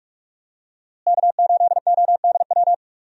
Morse code, heard as a keyed tone, about 44 wpm, 700 Hz. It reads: K8ODW
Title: USA Call Signs - 5 Characters - Repeated 3x - 45wpm